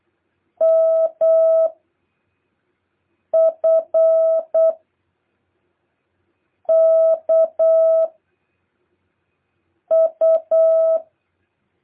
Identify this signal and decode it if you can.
Morse code, heard as a keyed tone, about 8 wpm, 650 Hz. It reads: M F K U
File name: jh8cbh / 2023cw